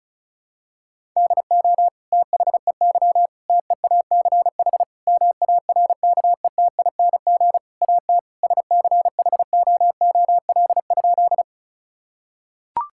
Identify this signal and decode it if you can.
Morse code, heard as a keyed tone, about 36 wpm, 700 Hz. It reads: DO THEY TEACH MARKETING AT SCHOOL?
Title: Sentences from Top 1000 Words - Review Part 5 - 35wpm